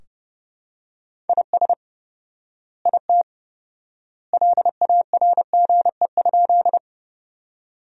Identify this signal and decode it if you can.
Morse code, heard as a keyed tone, about 30 wpm, 700 Hz. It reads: IS IT LARGE?
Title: 4x Word-Spacing - Sentences from Top 200 Words - 30wpm